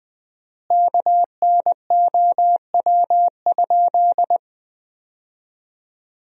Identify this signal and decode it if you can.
Morse code, heard as a keyed tone, about 20 wpm, 700 Hz. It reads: KNOW?
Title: 4x Word-Spacing - Sentences from Top 100 Words - 20wpm